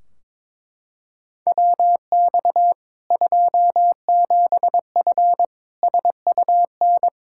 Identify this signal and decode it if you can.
Morse code, heard as a keyed tone, about 22 wpm, 700 Hz. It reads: WX 27F SUN
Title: Practice QSO - Exchange 2 - Info - 22wpm